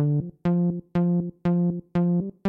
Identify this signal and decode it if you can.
There is no code.